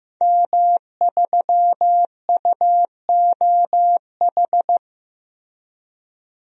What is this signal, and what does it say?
Morse code, heard as a keyed tone, about 15 wpm, 700 Hz. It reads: M3UOH
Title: DXCC-100 Call Signs - 5 Characters - Repeated 2x - 15wpm